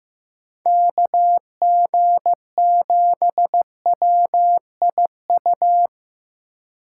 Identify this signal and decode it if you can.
Morse code, heard as a keyed tone, about 15 wpm, 700 Hz. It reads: KG7WIU
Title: USA Call Signs - 6 Characters - 2x Character Spacing - 15wpm